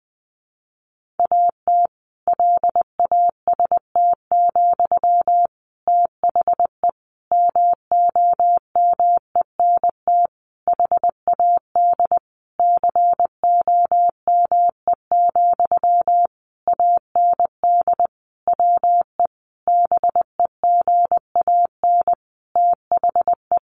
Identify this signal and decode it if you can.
Morse code, heard as a keyed tone, about 20 wpm, 700 Hz. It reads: AT LAST, THE MOMENT HAD COME, AND WE BEGAN THE
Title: Sentences from Top 1500 Words 20wpm